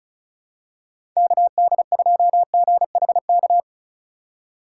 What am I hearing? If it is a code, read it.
Morse code, heard as a keyed tone, about 34 wpm, 700 Hz. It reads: KD2GHK
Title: USA Call Signs - 6 Characters - 2x Character Spacing - 35wpm